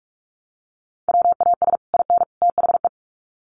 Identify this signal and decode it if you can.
Morse code, heard as a keyed tone, about 44 wpm, 700 Hz. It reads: WAS IN THE
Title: Sets of 3 Words 45wpm